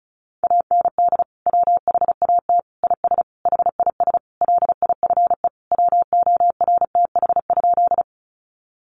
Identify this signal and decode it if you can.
Morse code, heard as a keyed tone, about 34 wpm, 700 Hz. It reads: AND WHAT IS HIS LIFE WORTH?
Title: Sentences from Top 800 Words 35wpm